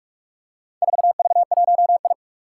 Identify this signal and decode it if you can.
Morse code, heard as a keyed tone, about 44 wpm, 700 Hz. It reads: 4V1I
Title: Four Challenging Letter-Number Combinations 45wpm